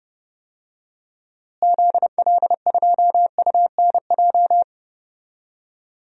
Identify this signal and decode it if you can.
Morse code, heard as a keyed tone, about 30 wpm, 700 Hz. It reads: ZL2UNJ
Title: DXCC-100 Call Signs - 6 Characters - Repeated 3x - 30wpm